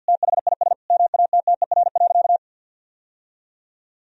Morse code, heard as a keyed tone, about 51 wpm, 700 Hz. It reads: THIS MATTER.
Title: Sentences from Top 1000 Words - Review Part 2 - 50wpm